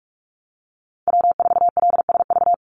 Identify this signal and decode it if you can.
Morse code, heard as a keyed tone, about 44 wpm, 700 Hz. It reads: W4LSV
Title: Call Signs - 45wpm